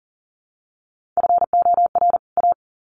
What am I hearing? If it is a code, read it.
Morse code, heard as a keyed tone, about 40 wpm, 700 Hz. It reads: FOR A
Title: Sets of 2 Words 40wpm